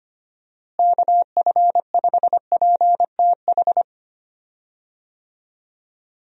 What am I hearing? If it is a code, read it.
Morse code, heard as a keyed tone, about 25 wpm, 700 Hz. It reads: KF5PTH
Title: USA Call Signs - 6 Characters - Repeated 2x - 25wpm